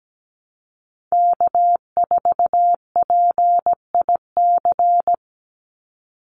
Morse code, heard as a keyed tone, about 17 wpm, 700 Hz. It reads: K4PIC
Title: NA SOTA Chaser Callsigns 17wpm